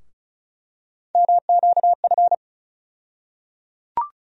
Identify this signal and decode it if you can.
Morse code, heard as a keyed tone, about 34 wpm, 700 Hz. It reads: MQF
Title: USA Call Sign Postfixes - 3 Letters 35wpm